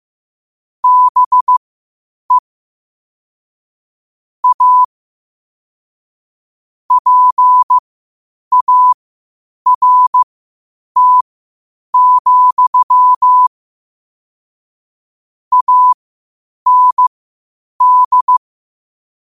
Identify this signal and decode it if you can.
Morse code, heard as a keyed tone, about 15 wpm, spaced as for 8 wpm, 1000 Hz. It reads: BE A PART, AND
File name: qotdfiles / 8wpm